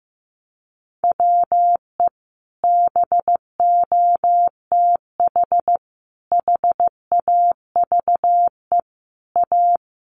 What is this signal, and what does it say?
Morse code, heard as a keyed tone, about 15 wpm, 700 Hz. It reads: WE BOTH HAVE A
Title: Sentences from Top 1200 Words 15wpm